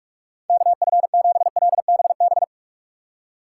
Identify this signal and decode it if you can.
Morse code, heard as a keyed tone, about 57 wpm, 700 Hz. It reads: KF7LBB